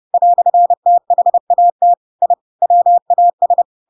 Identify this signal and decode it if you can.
Morse code, heard as a keyed tone, about 30 wpm, 700 Hz. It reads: "THAT I WAS
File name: Pinocchio / Pinocchio ch7 30 30 ik1hge